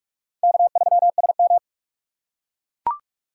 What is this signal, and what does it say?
Morse code, heard as a keyed tone, about 43 wpm, 700 Hz. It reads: K3SM
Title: USA Call Signs - 4 Characters - 4x Character Spacing - 45wpm